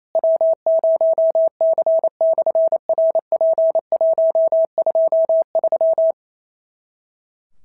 Morse code, heard as a keyed tone, about 28 wpm, 650 Hz. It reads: W0C/RP123
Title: SOTA QSO Practice 28wpm